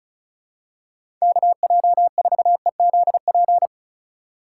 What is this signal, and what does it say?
Morse code, heard as a keyed tone, about 34 wpm, 700 Hz. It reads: KJ4EZP
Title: USA Call Signs - 6 Characters - 2x Character Spacing - 35wpm